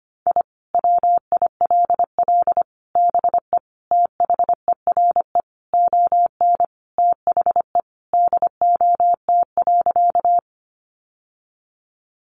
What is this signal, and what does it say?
Morse code, heard as a keyed tone, about 25 wpm, 700 Hz. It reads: I WILL BE THERE ON THE DOT.
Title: Sentences from Top 1500 Words 25wpm